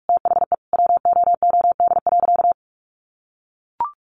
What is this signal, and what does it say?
Morse code, heard as a keyed tone, about 44 wpm, 700 Hz. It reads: THE WOOD.
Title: Sentences from Top 1200 Words 45wpm